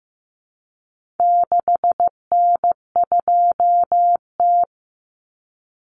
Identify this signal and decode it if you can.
Morse code, heard as a keyed tone, about 15 wpm, 700 Hz. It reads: 6N2T
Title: Common Cut Numbers - 4-Digit 15wpm